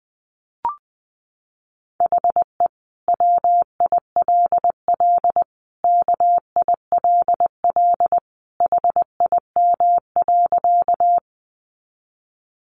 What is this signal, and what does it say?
Morse code, heard as a keyed tone, about 20 wpm, 700 Hz. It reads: HE WILL KILL HIM.
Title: Sentences from Top 1400 Words 20wpm